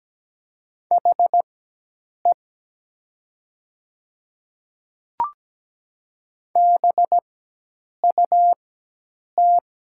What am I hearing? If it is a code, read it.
Morse code, heard as a keyed tone, about 17 wpm, spaced as for 8 wpm, 700 Hz. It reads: HE BUT
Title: 4x Character Spacing - Top 100 Words - 17wpm